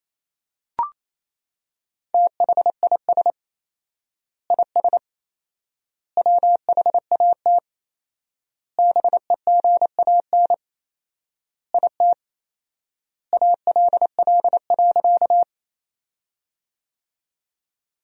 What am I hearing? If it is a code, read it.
Morse code, heard as a keyed tone, about 28 wpm, 700 Hz. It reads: THIS IS WHAT BEGAN IT ALL.
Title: 4x Word-Spacing - Sentences from Top 300 Words - 28wpm